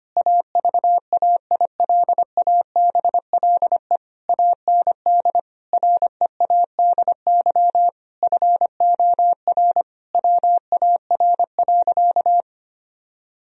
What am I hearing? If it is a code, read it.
Morse code, heard as a keyed tone, about 25 wpm, 700 Hz. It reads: AVAILABLE AND READY FOR WAR.
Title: Sentences from Top 1000 Words - Review Part 3 - 25wpm